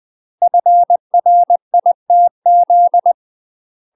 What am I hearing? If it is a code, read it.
Morse code, heard as a keyed tone, about 20 wpm, 700 Hz. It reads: FRITZ
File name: UK MASTER QSOs TRAINING / 20WPM / UK MASTER QSOs 0043